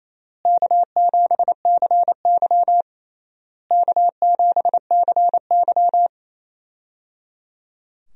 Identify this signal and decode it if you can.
Morse code, heard as a keyed tone, about 28 wpm, 700 Hz. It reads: K7CY K7CY
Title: USA Call Signs - 4 Characters - Repeated 3x - 28wpm